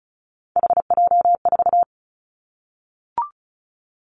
Morse code, heard as a keyed tone, about 34 wpm, 700 Hz. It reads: HJ4
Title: Callsign Prefixes for Top 100 DXCC - 35wpm